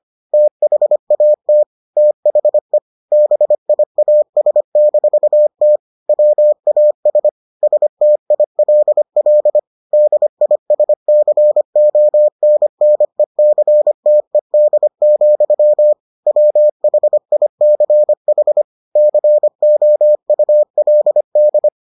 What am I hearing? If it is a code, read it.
Morse code, heard as a keyed tone, about 25 wpm, 600 Hz. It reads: THAT THE BIAS-T WAS STILL DISCONNECTED, WHICH COULD